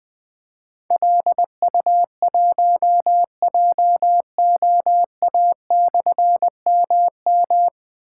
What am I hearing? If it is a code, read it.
Morse code, heard as a keyed tone, about 20 wpm, 700 Hz. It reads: LU1JOA/MM
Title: Difficult Call Signs 20wpm